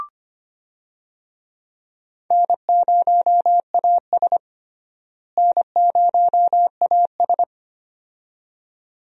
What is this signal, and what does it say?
Morse code, heard as a keyed tone, about 25 wpm, 700 Hz. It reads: N0AS N0AS
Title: USA Call Signs - 4 Characters - Repeated 2x - 25wpm